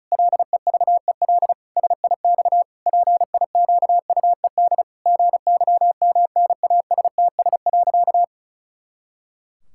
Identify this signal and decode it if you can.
Morse code, heard as a keyed tone, about 36 wpm, 700 Hz. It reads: LEVEL SIX PIQUED GYMNASTS.
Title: Pangram Challenge v2 - 35wpm